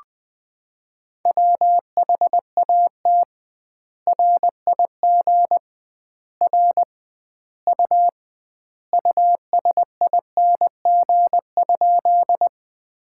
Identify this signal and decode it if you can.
Morse code, heard as a keyed tone, about 20 wpm, 700 Hz. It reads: WHAT RIG R U USING?
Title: Common QSO Phrases - 2x Word-Spacing - 20wpm